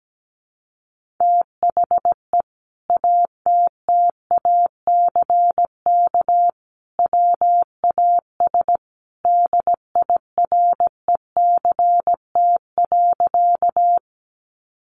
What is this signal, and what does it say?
Morse code, heard as a keyed tone, about 17 wpm, 700 Hz. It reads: THE ATTACK WAS DIRECT.
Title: Sentences from Top 1500 Words 17wpm